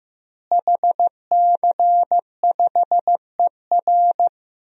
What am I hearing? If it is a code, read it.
Morse code, heard as a keyed tone, about 15 wpm, 700 Hz. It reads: HC5ER
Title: DXCC-100 Call Signs - 5 Characters - Repeated 2x - 15wpm